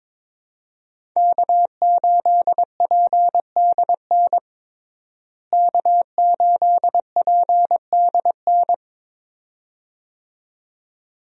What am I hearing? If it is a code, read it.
Morse code, heard as a keyed tone, about 21 wpm, 700 Hz. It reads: K8PDN K8PDN
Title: USA Call Signs - 5 Characters - Repeated 2x - 22wpm